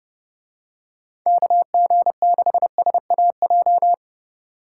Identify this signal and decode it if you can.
Morse code, heard as a keyed tone, about 29 wpm, 700 Hz. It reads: KG6SAJ